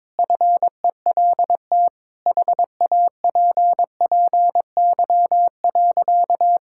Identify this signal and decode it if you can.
Morse code, heard as a keyed tone, about 22 wpm, 700 Hz. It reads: FELT HAPPY.
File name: practice / 22wpm / Sentences with 8-Word Phrases - 22wpm